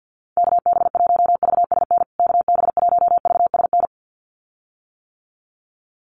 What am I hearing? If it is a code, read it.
Morse code, heard as a keyed tone, about 50 wpm, 700 Hz. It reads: KB1VSN KB1VSN
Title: USA Call Signs Repeated 2x 50wpm